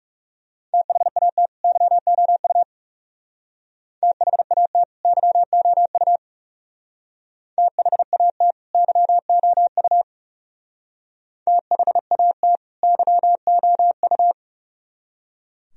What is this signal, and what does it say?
Morse code, heard as a keyed tone, about 35 wpm, 700 Hz. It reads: THAT YOU THAT YOU THAT YOU THAT YOU
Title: Speed-Racing - Sets of 2 Words 30wpm